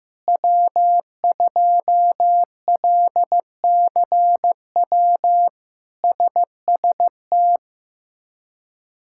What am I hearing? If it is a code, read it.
Morse code, heard as a keyed tone, about 15 wpm, 700 Hz. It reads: W2LCW SST
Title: K1USN SST QSO Practice - 15wpm